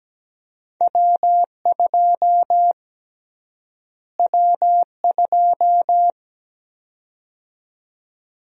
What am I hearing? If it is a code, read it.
Morse code, heard as a keyed tone, about 17 wpm, 700 Hz. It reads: W2 W2
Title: USA Call Sign Prefixes - Repeated 2x - 17wpm